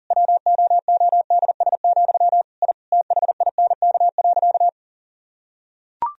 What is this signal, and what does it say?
Morse code, heard as a keyed tone, about 40 wpm, 700 Hz. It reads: WOODS, I THINK.
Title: Sentences from Top 1000 Words - Review Part 2 - 40wpm